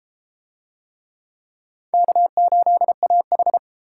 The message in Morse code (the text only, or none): K8AH